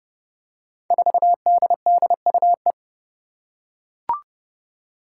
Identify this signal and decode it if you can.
Morse code, heard as a keyed tone, about 30 wpm, 700 Hz. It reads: 4DDUE
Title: Five Challenging Letter-Number Combinations 30wpm